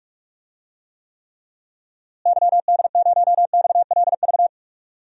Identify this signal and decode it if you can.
Morse code, heard as a keyed tone, about 46 wpm, 700 Hz. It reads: YD0XLV
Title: DXCC-100 Call Signs - 6 Characters - Repeated 3x - 45wpm